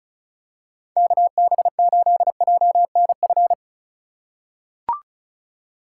Morse code, heard as a keyed tone, about 36 wpm, 700 Hz. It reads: KB8JNF
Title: USA Call Signs - 6 Characters - 6x Character Spacing - 35wpm